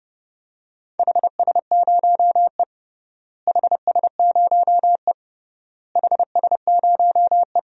Morse code, heard as a keyed tone, about 30 wpm, 700 Hz. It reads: HS0E HS0E HS0E
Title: DXCC-100 Call Signs - 4 Characters - Repeated 3x - 30wpm